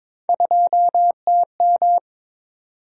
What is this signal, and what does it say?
Morse code, heard as a keyed tone, about 22 wpm, 700 Hz. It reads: 2TM
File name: practice / 22wpm / Three Letter-Number Combinations - 22wpm